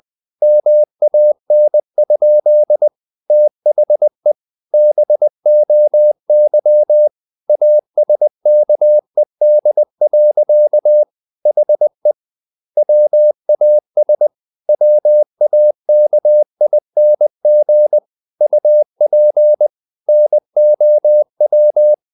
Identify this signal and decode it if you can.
Morse code, heard as a keyed tone, about 20 wpm, 600 Hz. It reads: MAN? THE BOY ASKED. HE WAS WAKING UP NOW
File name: the old man and the sea / the old man and the sea part 1 20wpm